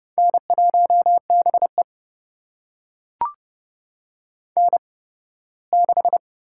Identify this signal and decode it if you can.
Morse code, heard as a keyed tone, about 30 wpm, spaced as for 18 wpm, 700 Hz. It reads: N1BE N6
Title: USA Call Signs - 4 Characters - 8x Character Spacing - 30wpm